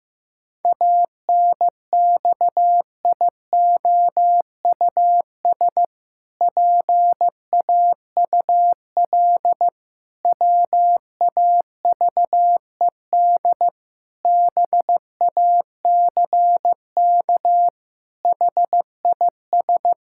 Morse code, heard as a keyed tone, about 16 wpm, 700 Hz. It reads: ANXIOUS PAUL WAVED BACK HIS
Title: Pangram Challenge v2 - 15wpm